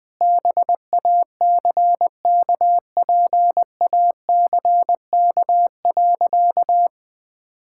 Morse code, heard as a keyed tone, about 20 wpm, 700 Hz. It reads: BACKPACK.
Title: Easy Sentences with Hints - 20wpm